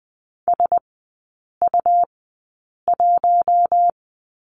Morse code, heard as a keyed tone, about 20 wpm, spaced as for 8 wpm, 690 Hz. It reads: SU1
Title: 191 Lesson 31e - Letter v - Three Characters 20wpm